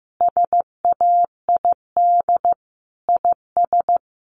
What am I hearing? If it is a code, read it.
Morse code, heard as a keyed tone, about 15 wpm, 700 Hz. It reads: SAID IS
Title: Sentences from Top 200 Words 15wpm